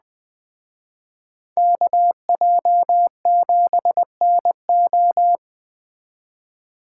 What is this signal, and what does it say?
Morse code, heard as a keyed tone, about 20 wpm, 700 Hz. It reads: KJ7NO
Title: Speed Racing - USA 2x2 Call Signs 17wpm